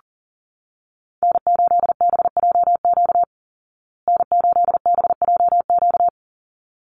Morse code, heard as a keyed tone, about 40 wpm, 700 Hz. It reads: N8BJQ N8BJQ
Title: Call Signs Repeated 3x - 40wpm